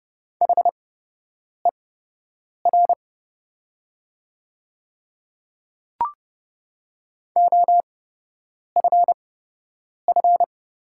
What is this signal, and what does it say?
Morse code, heard as a keyed tone, about 30 wpm, spaced as for 8 wpm, 700 Hz. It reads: HER OFF
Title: 8x Character Spacing - Top 500 Words - 30wpm